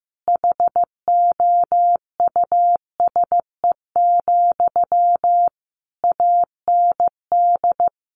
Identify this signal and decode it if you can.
Morse code, heard as a keyed tone, about 15 wpm, 700 Hz. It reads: HOUSE, AND
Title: Sentences from Top 1400 Words 15wpm